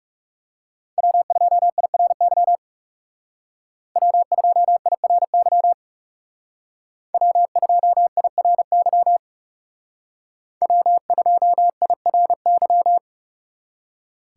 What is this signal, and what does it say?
Morse code, heard as a keyed tone, about 38 wpm, 700 Hz. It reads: W2IRY W2IRY W2IRY W2IRY